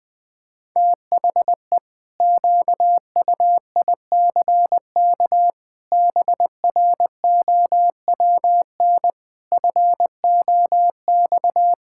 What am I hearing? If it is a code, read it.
Morse code, heard as a keyed tone, about 20 wpm, 700 Hz. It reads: THE QUICK BROWN FOX
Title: Sentences with 8-Word Phrases - Spoken Beginning - 20wpm